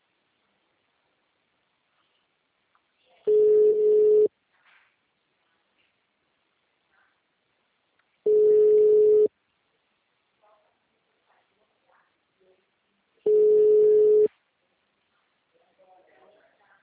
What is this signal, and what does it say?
Ringback tone (Europe, 425 Hz), with speech.